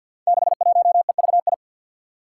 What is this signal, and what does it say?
Morse code, heard as a keyed tone, about 52 wpm, 700 Hz. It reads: 61EVI
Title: Five Challenging Letter-Number Combinations 50wpm